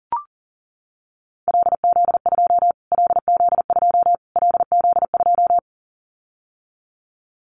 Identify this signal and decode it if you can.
Morse code, heard as a keyed tone, about 40 wpm, 700 Hz. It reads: LZ2 LZ2 LZ2
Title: Callsign Prefixes for Top 25 DXCC - Repeated 3x - 40wpm